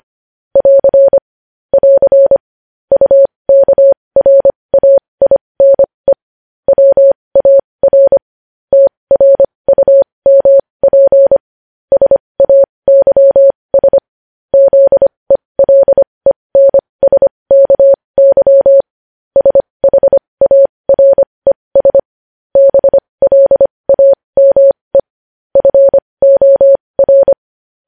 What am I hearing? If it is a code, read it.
Morse code, heard as a keyed tone, about 25 wpm, spaced as for 20 wpm, 550 Hz. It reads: <AR> <AR> UKRAINE WAR TRUMP SAYS ZELENSKY SHARES BLAME FOR
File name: podcast / News headlines-1x-25x20